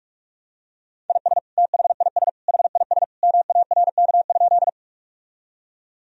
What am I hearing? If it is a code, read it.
Morse code, heard as a keyed tone, about 46 wpm, 700 Hz. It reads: IS THIS HIS MARK?
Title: Sentences from Top 1000 Words - Review Part 1 - 45wpm